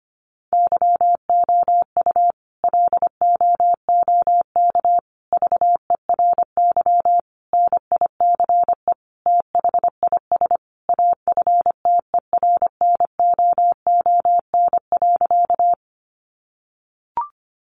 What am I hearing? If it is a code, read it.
Morse code, heard as a keyed tone, about 25 wpm, 700 Hz. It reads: YOU LOOK VERY NICE THIS AFTERNOON.
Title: Sentences from Top 1500 Words 25wpm